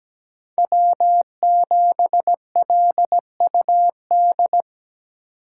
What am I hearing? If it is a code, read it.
Morse code, heard as a keyed tone, about 18 wpm, 700 Hz. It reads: W7LUD
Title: Speed Racing - USA 1x3 Call Signs 17wpm